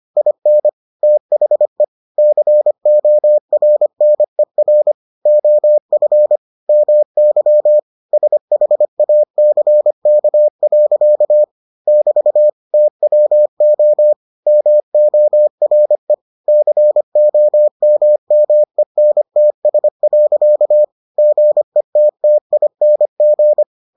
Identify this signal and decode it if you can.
Morse code, heard as a keyed tone, about 25 wpm, 600 Hz. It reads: IN THE CORNER OF MY SHACK. <BT> TWO MORE COMMENTS. GETTING